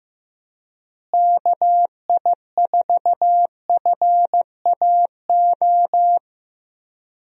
Morse code, heard as a keyed tone, about 15 wpm, 700 Hz. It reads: KI4FAO
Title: USA Call Signs - 6 Characters - Repeated 3x - 15wpm